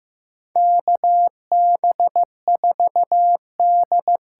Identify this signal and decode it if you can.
Morse code, heard as a keyed tone, about 15 wpm, 700 Hz. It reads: KB4D
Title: USA Call Signs - 6 Characters - 8x Character Spacing - 15wpm